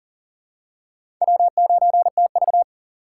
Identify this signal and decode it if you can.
Morse code, heard as a keyed tone, about 40 wpm, 700 Hz. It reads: W9TV
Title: USA Call Signs - 4 Characters - 4x Character Spacing - 40wpm